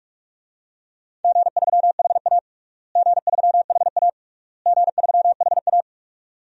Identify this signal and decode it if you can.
Morse code, heard as a keyed tone, about 46 wpm, 700 Hz. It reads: G3HA G3HA G3HA
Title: DXCC-100 Call Signs - 4 Characters - Repeated 3x - 45wpm